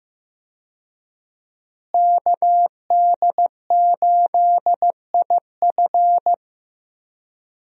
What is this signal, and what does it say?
Morse code, heard as a keyed tone, about 16 wpm, 700 Hz. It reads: KD8IF
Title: USA Call Signs - 5 Characters - Repeated 2x - 15wpm